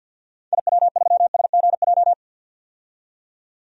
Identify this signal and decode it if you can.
Morse code, heard as a keyed tone, about 48 wpm, 700 Hz. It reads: IW3SGJ
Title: DXCC-100 Call Signs - 6 Characters - 50wpm